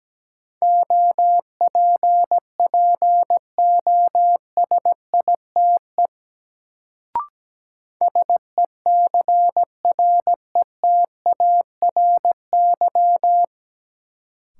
Morse code, heard as a keyed tone, about 17 wpm, 700 Hz. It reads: OPPOSITE SECRETARY
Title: Sentences from Top 1300 Words - New Word Drill Alt - 17wpm